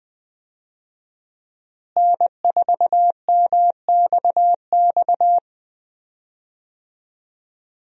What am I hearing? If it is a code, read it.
Morse code, heard as a keyed tone, about 20 wpm, 700 Hz. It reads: N4MXX
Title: USA Call Signs - 5 Characters - Repeated 3x - 20wpm